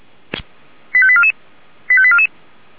Five-tone selective call (ZVEI), heard as quick runs of five tones, tones of about 70 ms. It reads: 85844 85844